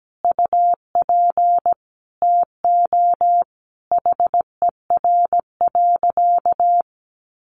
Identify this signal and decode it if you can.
Morse code, heard as a keyed tone, about 17 wpm, 700 Hz. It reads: UP TO HER.
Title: Additional Easy Phrases 17wpm